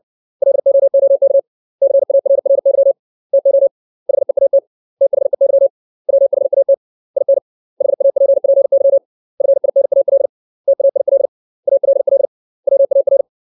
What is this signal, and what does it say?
Morse code, heard as a keyed tone, about 60 wpm, 550 Hz. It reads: LOOK CARRY TO HEAT THEY WHAT IN HAPPY FINAL TAIL ALL WAR
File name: 632 words / 632 Words 60WPM